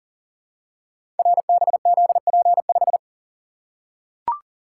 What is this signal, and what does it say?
Morse code, heard as a keyed tone, about 38 wpm, 690 Hz. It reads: RBZP5
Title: Code Groups - 5 Letter:Numbers - 40wpm